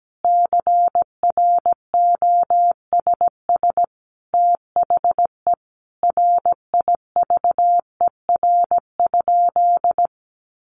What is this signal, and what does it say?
Morse code, heard as a keyed tone, about 17 wpm, 700 Hz. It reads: CROSS THE RIVER?
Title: Sentences from Top 1400 Words 17wpm